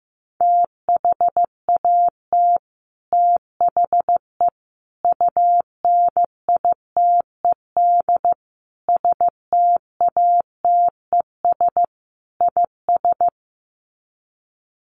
Morse code, heard as a keyed tone, about 15 wpm, 700 Hz. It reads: THAT THE UNITED STATES IS
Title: Sets of 5 Words Encore 15wpm